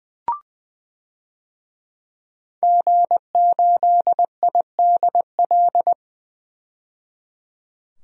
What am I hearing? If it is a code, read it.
Morse code, heard as a keyed tone, about 21 wpm, 700 Hz. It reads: G8IDL
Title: DXCC-100 Call Signs - 5 Characters - 20wpm